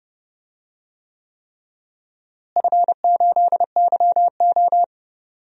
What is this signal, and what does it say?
Morse code, heard as a keyed tone, about 29 wpm, 700 Hz. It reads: F8YO
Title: DXCC-100 Call Signs - 4 Characters - Repeated 3x - 30wpm